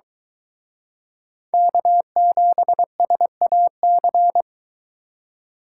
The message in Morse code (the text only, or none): K7SAC